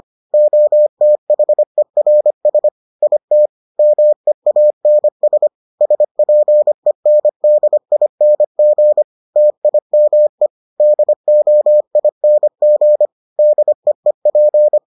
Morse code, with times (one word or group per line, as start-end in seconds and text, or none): 0.3-2.7 OTHERS
3.0-3.5 IT
3.8-5.5 MEANS
5.8-9.0 SPENDING
9.4-10.5 TIME
10.8-13.1 DOING
13.4-14.8 DEEP